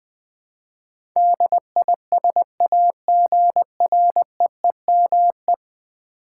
Morse code, heard as a keyed tone, about 20 wpm, 700 Hz. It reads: DISAGREEME
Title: Build A Word - 5 Letters At A Time - 12-Letter Words - 20wpm